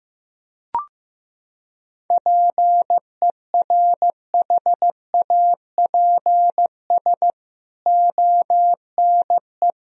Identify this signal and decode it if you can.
Morse code, heard as a keyed tone, about 15 wpm, 700 Hz. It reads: PERHAPS ONE
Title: Sentences from Top 1000 Words - Review Part 1 - 15wpm